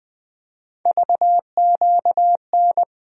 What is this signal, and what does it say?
Morse code, heard as a keyed tone, about 20 wpm, 700 Hz. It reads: VQN
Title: USA Call Sign Postfixes - 3 Letters 20wpm